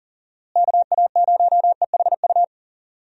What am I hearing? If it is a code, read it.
Morse code, heard as a keyed tone, about 40 wpm, 700 Hz. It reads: KA0EHU
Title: USA Call Signs - 6 Characters - Repeated 3x - 40wpm